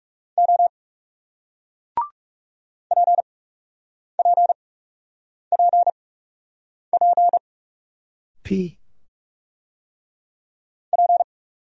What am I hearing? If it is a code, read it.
Morse code, heard as a keyed tone, about 38 wpm, 700 Hz. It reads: O P P P P P